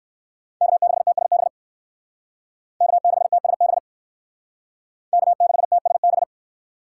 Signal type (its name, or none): Morse code